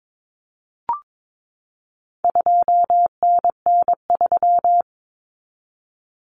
Morse code, heard as a keyed tone, about 22 wpm, 700 Hz. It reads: 2NN3
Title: Common Cut Numbers - 4-Digit 22wpm